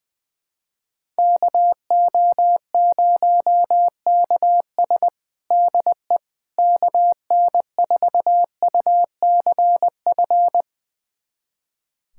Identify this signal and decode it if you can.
Morse code, heard as a keyed tone, about 21 wpm, 700 Hz. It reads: KO0KS DE KN4UCF
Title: USA Callsign DE Callsign 20wpm